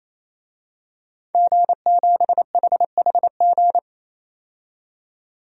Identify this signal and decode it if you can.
Morse code, heard as a keyed tone, about 28 wpm, 700 Hz. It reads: G7HHG
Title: DXCC-100 Call Signs - 5 Characters - 28wpm